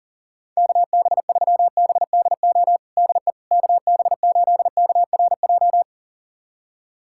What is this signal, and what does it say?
Morse code, heard as a keyed tone, about 40 wpm, 700 Hz. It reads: KB3BDO DE KB8KRJ
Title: USA Callsign DE Callsign 40wpm